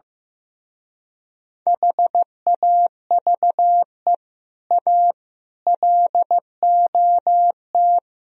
Morse code, heard as a keyed tone, about 15 wpm, 700 Hz. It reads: HAVE A LOT
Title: Speed-Racing - Sets of 4 Words 15wpm